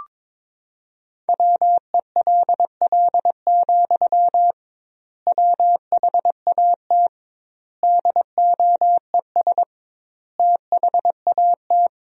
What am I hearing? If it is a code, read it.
Morse code, heard as a keyed tone, about 22 wpm, 700 Hz. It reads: WELL, WHAT DOES THAT
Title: 2x Word-Spacing - Sentences from Top 500 Words - 22wpm